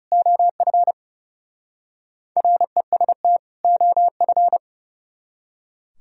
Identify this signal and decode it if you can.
Morse code, heard as a keyed tone, about 39 wpm, 700 Hz. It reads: OF REST OF